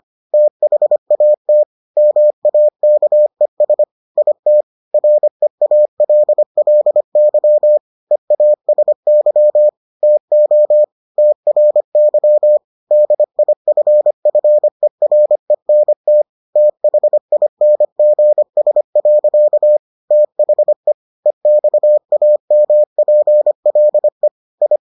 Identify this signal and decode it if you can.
Morse code, heard as a keyed tone, about 25 wpm, 600 Hz. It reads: THAT MAKES IT REALLY EASY TO TRY DIFFERENT THINGS. THE EXAMPLE I